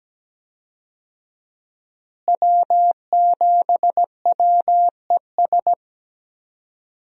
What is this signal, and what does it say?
Morse code, heard as a keyed tone, about 17 wpm, 700 Hz. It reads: W7WES
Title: USA Call Signs - 5 Characters - Repeated 2x - 17wpm